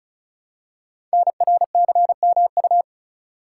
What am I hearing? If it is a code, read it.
Morse code, heard as a keyed tone, about 36 wpm, 700 Hz. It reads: NRCMU
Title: Code Groups - 5 Letters - 35wpm